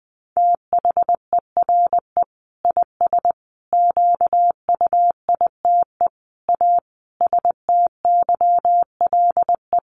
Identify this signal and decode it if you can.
Morse code, heard as a keyed tone, about 20 wpm, 700 Hz. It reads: THERE IS QUITE A STYLE
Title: Sentences from Top 800 Words 20wpm